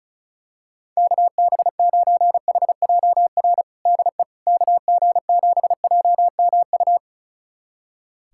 Morse code, heard as a keyed tone, about 36 wpm, 700 Hz. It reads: KB9HJR DE KG7JMU
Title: USA Callsign DE Callsign - 6 Characters - 35wpm